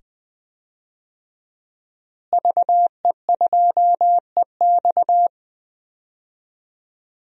Morse code, heard as a keyed tone, about 20 wpm, 700 Hz. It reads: VE2EX
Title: DXCC-100 Call Signs - 5 Characters - Repeated 2x - 20wpm